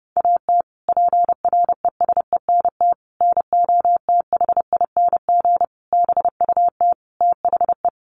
Morse code, heard as a keyed tone, about 30 wpm, 700 Hz. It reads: AT PRESENT NOTHING BUT THE
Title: Sentences from Top 1500 Words 30wpm